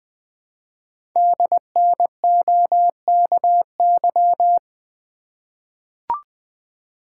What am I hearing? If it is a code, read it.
Morse code, heard as a keyed tone, about 20 wpm, 700 Hz. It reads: DNOKY
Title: Code Groups - 5 Letters - 20wpm